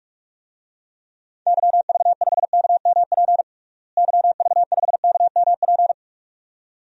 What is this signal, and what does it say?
Morse code, heard as a keyed tone, about 46 wpm, 700 Hz. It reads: YV5KMP YV5KMP
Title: DXCC-100 Call Signs - 6 Characters - Repeated 2x - 45wpm